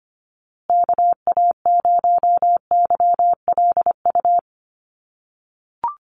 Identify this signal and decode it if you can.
Morse code, heard as a keyed tone, about 25 wpm, 700 Hz. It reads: KA0YLU
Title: USA Call Signs Repeated 2x 25wpm